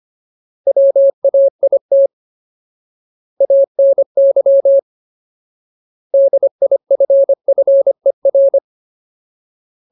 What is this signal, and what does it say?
Morse code, heard as a keyed tone, about 25 wpm, 550 Hz. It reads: WAIT ANY DIFFER
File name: Ultimate Word List / 25WPM Ultimate